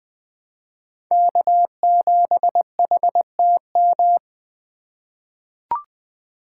Morse code, heard as a keyed tone, about 20 wpm, 700 Hz. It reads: K7HTM
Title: 2023 POTA Activator Call Signs - 20wpm